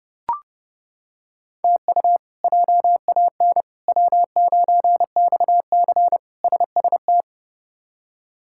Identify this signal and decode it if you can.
Morse code, heard as a keyed tone, about 29 wpm, 700 Hz. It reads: TU JAN W9XC SST
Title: K1USN SST QSO Practice - 30wpm